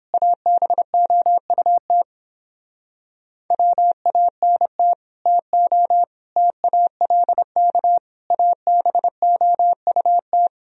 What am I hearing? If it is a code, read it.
Morse code, heard as a keyed tone, about 29 wpm, 700 Hz. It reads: ABOUT WANT TO TALK ABOUT